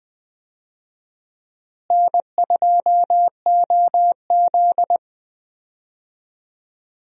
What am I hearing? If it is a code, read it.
Morse code, heard as a keyed tone, about 20 wpm, 700 Hz. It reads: N2OZ
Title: USA Call Signs - 4 Characters - Repeated 2x - 20wpm